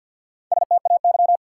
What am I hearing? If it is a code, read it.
Morse code, heard as a keyed tone, about 53 wpm, 700 Hz. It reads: STAY
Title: Four Letter Words - 50wpm